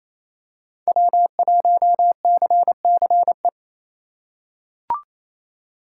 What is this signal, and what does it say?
Morse code, heard as a keyed tone, about 28 wpm, 700 Hz. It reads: W1CCE
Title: K1USN SST QSO Practice - 28wpm